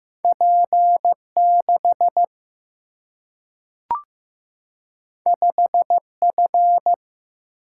Morse code, heard as a keyed tone, about 15 wpm, 700 Hz. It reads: P 6 5 F
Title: Single-Letter Single-Number 15wpm